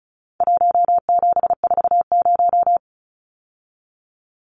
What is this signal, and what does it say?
Morse code, heard as a keyed tone, about 34 wpm, 700 Hz. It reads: 1740
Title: Numbers - 4-Digit 35wpm